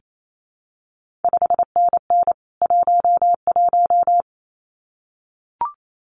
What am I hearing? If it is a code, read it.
Morse code, heard as a keyed tone, about 28 wpm, 700 Hz. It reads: 5NN 11
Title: CQ WW DX Contest Exchange Encore - 28wpm